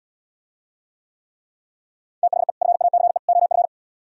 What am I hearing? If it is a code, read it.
Morse code, heard as a keyed tone, about 77 wpm, 700 Hz. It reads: THE FACE OF